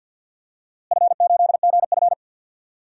Morse code, heard as a keyed tone, about 52 wpm, 700 Hz. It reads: F8GF
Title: DXCC-100 Call Signs - 4 Characters - Repeated 3x - 50wpm